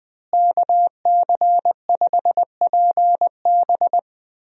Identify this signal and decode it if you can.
Morse code, heard as a keyed tone, about 20 wpm, 700 Hz. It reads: KC5PB
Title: USA Call Signs - 5 Characters - Repeated 3x - 20wpm